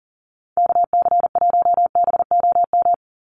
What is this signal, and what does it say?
Morse code, heard as a keyed tone, about 40 wpm, 700 Hz. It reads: KC1BOM
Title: USA Call Signs 40wpm